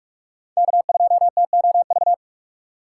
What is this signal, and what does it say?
Morse code, heard as a keyed tone, about 43 wpm, 700 Hz. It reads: K2TOV
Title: USA Call Signs - 5 Characters - 6x Character Spacing - 45wpm